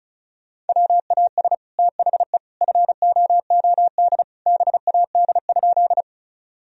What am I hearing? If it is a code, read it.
Morse code, heard as a keyed tone, about 34 wpm, 700 Hz. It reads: WAS THE FOOD BAD?
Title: Sentences from Top 1000 Words - Review Part 1 - 35wpm